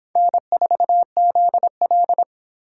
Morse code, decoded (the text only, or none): N4ZL